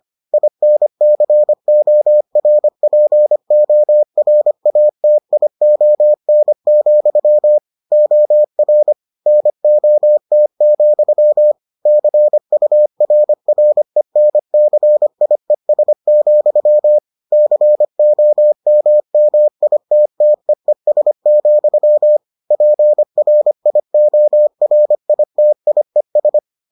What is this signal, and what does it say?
Morse code, heard as a keyed tone, about 25 wpm, 600 Hz. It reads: INCORPORATION, OR NOT, CURRENCIES, COMMITTEES, PRIORITIES